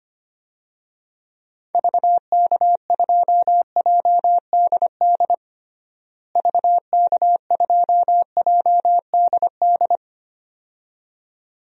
Morse code, heard as a keyed tone, about 25 wpm, 700 Hz. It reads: VK2JDD VK2JDD
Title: DXCC-100 Call Signs - 6 Characters - Repeated 2x - 25wpm